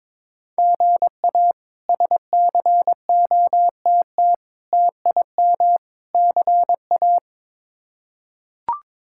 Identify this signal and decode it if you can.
Morse code, heard as a keyed tone, about 21 wpm, 700 Hz. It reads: GA SCOTT TIM CA
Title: K1USN SST QSO Practice - 22wpm